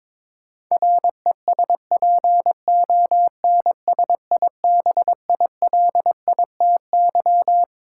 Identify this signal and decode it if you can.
Morse code, heard as a keyed tone, about 21 wpm, 700 Hz. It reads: RESPONSIBILITY
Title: Sentences from Top 1200 Words - New Word Drill Alt - 22wpm